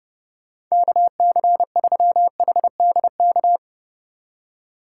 Morse code, heard as a keyed tone, about 30 wpm, 700 Hz. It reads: KC3HDK
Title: USA Call Signs - 6 Characters - 2x Character Spacing - 30wpm